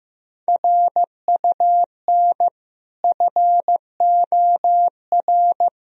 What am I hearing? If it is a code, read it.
Morse code, heard as a keyed tone, about 16 wpm, 700 Hz. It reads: RUN FOR
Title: Sentences from Top 1000 Words - Review Part 1 - 15wpm